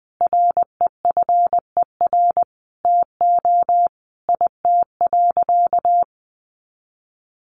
Morse code, heard as a keyed tone, about 19 wpm, 700 Hz. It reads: REFER TO IT.
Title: Sentences from Top 1000 Words 20wpm